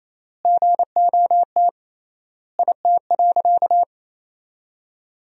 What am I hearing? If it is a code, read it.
Morse code, heard as a keyed tone, about 28 wpm, 700 Hz. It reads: GOT IT.
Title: 3x Word-Spacing - Sentences from Top 300 Words - 28wpm